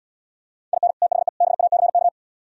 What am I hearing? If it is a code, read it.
Morse code, heard as a keyed tone, about 79 wpm, 700 Hz. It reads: IN THE BACK